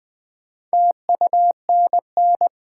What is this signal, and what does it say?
Morse code, heard as a keyed tone, about 20 wpm, 700 Hz. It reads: TUNN
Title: Build A Word - 2 Letters At A Time - 6-Letter Words - 20wpm